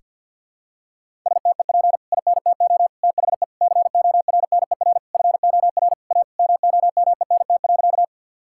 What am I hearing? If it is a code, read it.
Morse code, heard as a keyed tone, about 50 wpm, 700 Hz. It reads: STEP INTO THE CORNER FOR A MOMENT.
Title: Sentences from Top 1600 Words 50wpm